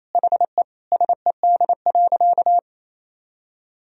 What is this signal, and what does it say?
Morse code, heard as a keyed tone, about 28 wpm, 700 Hz. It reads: HE SED.
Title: Sentences from Top 100 Words and Abbreviations 28wpm